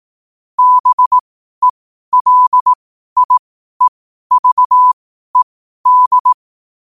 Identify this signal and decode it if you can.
Morse code, heard as a keyed tone, about 18 wpm, spaced as for 12 wpm, 1000 Hz. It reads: BELIEVED
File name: qotdfiles / 12wpm